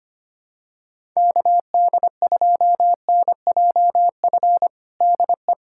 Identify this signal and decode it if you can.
Morse code, heard as a keyed tone, about 25 wpm, 700 Hz. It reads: KD2NJF DE
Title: USA Callsign DE Callsign - 6 Characters - 25wpm